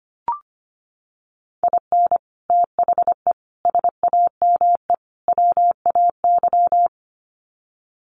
Morse code, heard as a keyed tone, about 25 wpm, 700 Hz. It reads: IN THE SAME WAY
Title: Sets of 4 Words 25wpm